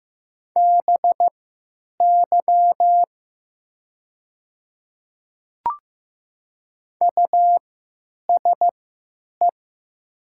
Morse code, heard as a keyed tone, about 15 wpm, spaced as for 9 wpm, 700 Hz. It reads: BY USE
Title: 3x Character Spacing - Top 100 Words - 15wpm